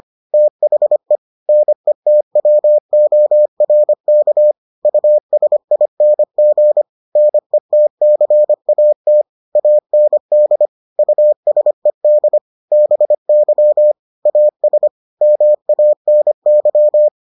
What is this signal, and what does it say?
Morse code, heard as a keyed tone, about 25 wpm, 600 Hz. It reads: THE NETWORK USING NETCAT AND USED BY AS MANY